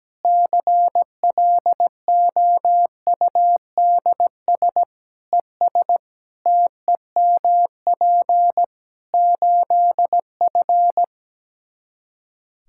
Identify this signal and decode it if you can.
Morse code, heard as a keyed tone, about 17 wpm, 700 Hz. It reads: CLOUDS ES TEMP 8F
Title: Practice QSO - Exchange 2 - Info - 17wpm